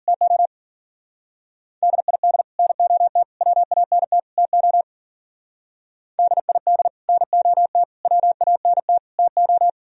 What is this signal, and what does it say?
Morse code, heard as a keyed tone, about 50 wpm, 700 Hz. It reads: TO DID NOT WANT TO DID NOT WANT TO